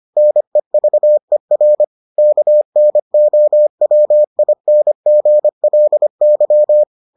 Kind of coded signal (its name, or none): Morse code